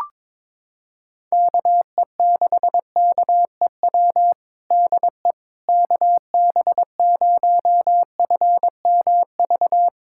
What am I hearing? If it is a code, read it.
Morse code, heard as a keyed tone, about 21 wpm, 700 Hz. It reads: KE6KEW DE KB0FMV
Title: USA Callsign DE Callsign 22wpm